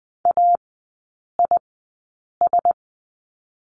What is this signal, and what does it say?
Morse code, heard as a keyed tone, about 20 wpm, spaced as for 8 wpm, 700 Hz. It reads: AIS
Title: 035 Lesson 7e - Letter s - Three Characters 20wpm